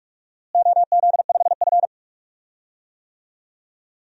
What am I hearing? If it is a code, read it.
Morse code, heard as a keyed tone, about 46 wpm, 700 Hz. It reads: OZ5F
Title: DXCC-100 Call Signs - 4 Characters - Repeated 3x - 45wpm